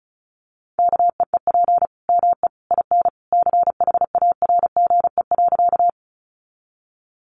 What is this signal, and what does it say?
Morse code, heard as a keyed tone, about 34 wpm, 700 Hz. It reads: KEEP ME IN CHARGE.
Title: Sentences from Top 900 Words 35wpm